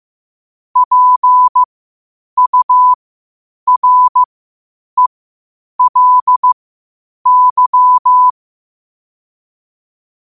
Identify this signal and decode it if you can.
Morse code, heard as a keyed tone, about 15 wpm, spaced as for 8 wpm, 1000 Hz. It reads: PURELY
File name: qotdfiles / 8wpm